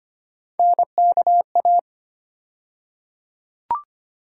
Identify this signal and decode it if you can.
Morse code, heard as a keyed tone, about 25 wpm, 700 Hz. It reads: NKA
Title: USA Call Sign Postfixes - 3 Letters 25wpm